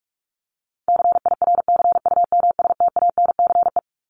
Morse code, heard as a keyed tone, about 44 wpm, 700 Hz. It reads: CIRCUMSTANCE
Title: ES:S Suffix - Top 100 Words with Base Words 45wpm